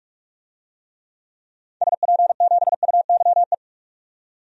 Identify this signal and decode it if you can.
Morse code, heard as a keyed tone, about 46 wpm, 700 Hz. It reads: SP7UYE
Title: DXCC-100 Call Signs - 6 Characters - 45wpm